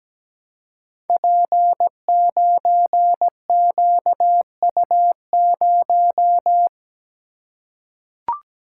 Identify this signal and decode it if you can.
Morse code, heard as a keyed tone, about 17 wpm, 700 Hz. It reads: P9QU0
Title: Code Groups - 5 Letter:Numbers - 17wpm